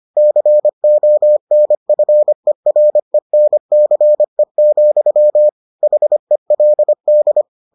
Morse code, heard as a keyed tone, about 25 wpm, 600 Hz. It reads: CONFERENCE, HELD